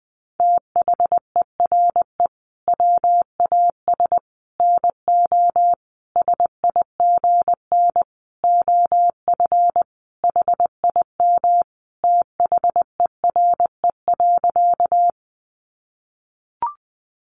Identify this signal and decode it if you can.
Morse code, heard as a keyed tone, about 19 wpm, 700 Hz. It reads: THERE WAS NO SIGN OF HIM THERE.
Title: Sentences from Top 1000 Words 20wpm